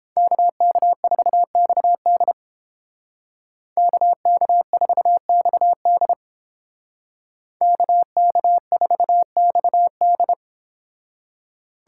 Morse code, heard as a keyed tone, about 29 wpm, 700 Hz. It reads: KK4XD KK4XD KK4XD